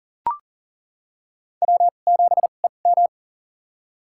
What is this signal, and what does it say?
Morse code, heard as a keyed tone, about 40 wpm, spaced as for 29 wpm, 700 Hz. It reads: W7EM
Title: USA Call Signs - 2x Character Spacing - 40wpm